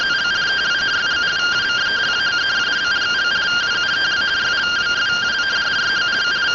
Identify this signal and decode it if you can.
Two-tone teleprinter signal, about 170 Hz shift.